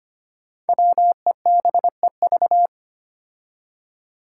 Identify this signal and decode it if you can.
Morse code, heard as a keyed tone, about 25 wpm, 700 Hz. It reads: WEBEV
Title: Five Challenging Letter-Number Combinations 25wpm